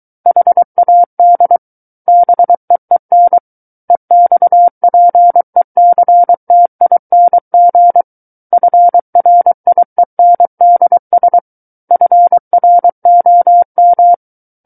Morse code, heard as a keyed tone, about 23 wpm, 700 Hz. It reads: HAD BEEN EXPECTING FRIENDS FROM
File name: Book One / 14-Chapter 14 - 23 WPM